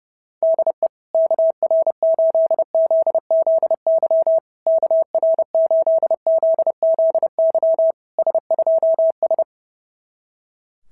Morse code, heard as a keyed tone, about 30 wpm, 690 Hz. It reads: DE KR8ZZY KR8ZZY S2S
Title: SOTA QSO Practice 30wpm